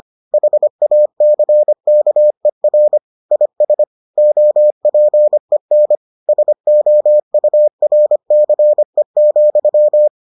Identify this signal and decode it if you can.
Morse code, heard as a keyed tone, about 25 wpm, 600 Hz. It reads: HACKER IS OPEN SOURCE,